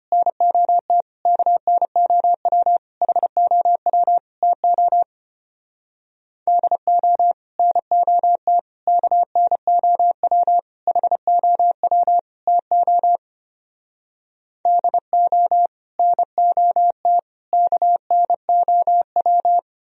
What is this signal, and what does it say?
Morse code, heard as a keyed tone, about 33 wpm, 700 Hz. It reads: NOT KNOW HOW TO DO NOT KNOW HOW TO DO NOT KNOW